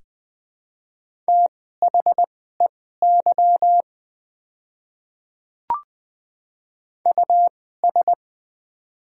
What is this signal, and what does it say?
Morse code, heard as a keyed tone, about 20 wpm, spaced as for 14 wpm, 700 Hz. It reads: THEY US
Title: 2x Character Spacing - Top 300 Words - 20wpm